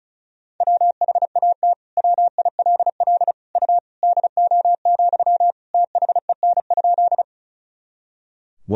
Morse code, heard as a keyed tone, about 36 wpm, 700 Hz. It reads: WHAT WILL U DO, THEN?
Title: Sentences from Top 100 Words and Abbreviations 35wpm